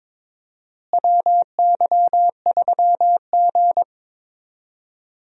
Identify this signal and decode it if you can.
Morse code, heard as a keyed tone, about 22 wpm, 700 Hz. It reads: WY3G
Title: USA Call Signs - 4 Characters - Repeated 3x - 22wpm